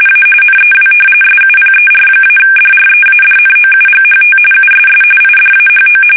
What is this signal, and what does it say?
RTTY, about 45.45 baud. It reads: A NEW SUPER-DREADNOUGHT USS IDAHO
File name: fsk850 / battleships